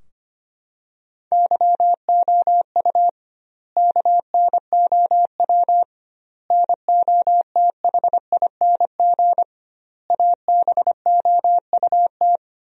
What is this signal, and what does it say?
Morse code, heard as a keyed tone, about 25 wpm, 700 Hz. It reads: YOU KNOW NOTHING ABOUT
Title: 2x Word-Spacing - Sentences from Top 500 Words - 25wpm